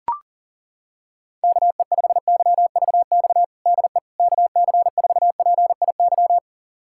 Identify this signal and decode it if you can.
Morse code, heard as a keyed tone, about 40 wpm, 690 Hz. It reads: KE5YVX DE KC4PIY
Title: USA Callsign DE Callsign - 6 Characters - 40wpm